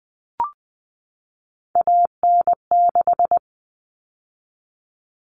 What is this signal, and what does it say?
Morse code, heard as a keyed tone, about 20 wpm, 700 Hz. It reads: AN6
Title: All Cut Numbers - 3-Digit 20wpm